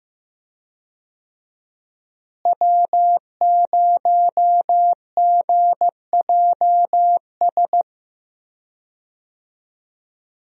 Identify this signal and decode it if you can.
Morse code, heard as a keyed tone, about 15 wpm, 700 Hz. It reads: W0GJS
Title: USA Call Signs - 5 Characters - Repeated 3x - 15wpm